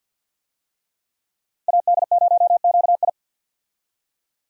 Morse code, heard as a keyed tone, about 50 wpm, 700 Hz. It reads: AD0QI
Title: USA Call Signs - 5 Characters - 50wpm